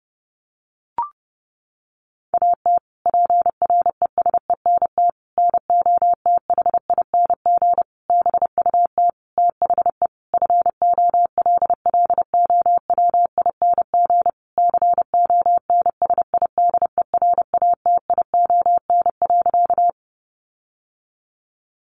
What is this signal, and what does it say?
Morse code, heard as a keyed tone, about 30 wpm, 700 Hz. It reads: AT PRESENT NOTHING BUT THE FOLLOWING CONSIDERATION.
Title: Sentences from Top 1500 Words 30wpm